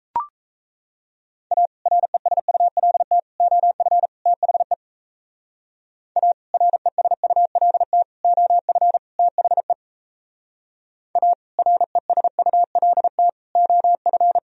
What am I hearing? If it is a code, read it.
Morse code, heard as a keyed tone, about 35 wpm, 700 Hz. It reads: A RESULT OF THE A RESULT OF THE A RESULT OF
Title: Speed-Racing - Sets of 4 Words 28wpm